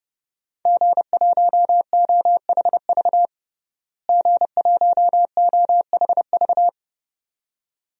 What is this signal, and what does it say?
Morse code, heard as a keyed tone, about 29 wpm, 700 Hz. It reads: G1OHV G1OHV
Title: DXCC-100 Call Signs - 5 Characters - Repeated 2x - 30wpm